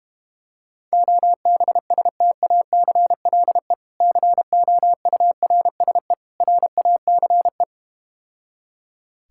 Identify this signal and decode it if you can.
Morse code, heard as a keyed tone, about 32 wpm, 690 Hz. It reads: OBSTACLE COURSE RACE